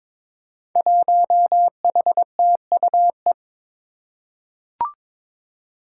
Morse code, heard as a keyed tone, about 21 wpm, 700 Hz. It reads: 1HTUE
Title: Five Challenging Letter-Number Combinations 22wpm